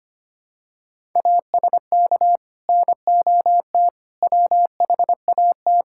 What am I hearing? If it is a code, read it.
Morse code, heard as a keyed tone, about 25 wpm, 700 Hz. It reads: ASK NOT WHAT
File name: practice / 25wpm / Sentences with 9-Word Phrases - 25wpm